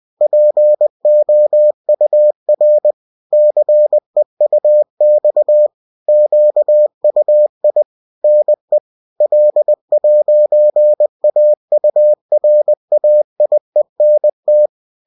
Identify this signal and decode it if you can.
Morse code, heard as a keyed tone, about 20 wpm, 600 Hz. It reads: POUR CEUX QUI NE L'AURAIENT